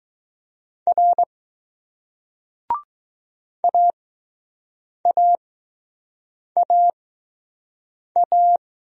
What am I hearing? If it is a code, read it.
Morse code, heard as a keyed tone, about 20 wpm, spaced as for 6 wpm, 700 Hz. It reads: R AAAA